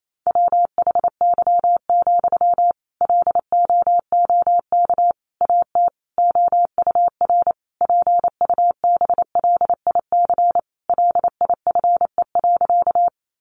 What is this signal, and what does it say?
Morse code, heard as a keyed tone, about 28 wpm, 700 Hz. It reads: WHY, LOOK AT OUR PUBLIC LIFE.
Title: Sentences from Top 600 Words 28wpm